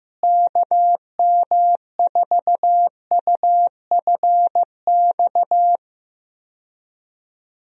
Morse code, heard as a keyed tone, about 15 wpm, 700 Hz. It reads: KM4UFX
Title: USA Call Signs - 6 Characters - Repeated 2x - 15wpm